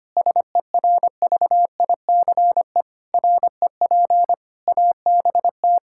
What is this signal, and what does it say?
Morse code, heard as a keyed tone, about 25 wpm, 700 Hz. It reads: SERVICE REP ABT
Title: Practice QSO - Exchange 3 - More Info - 25wpm